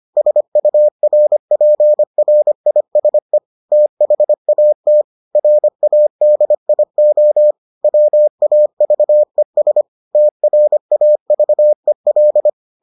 Morse code, heard as a keyed tone, about 25 wpm, 600 Hz. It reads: SURPRISE THAT RADIO WAVES TRAVEL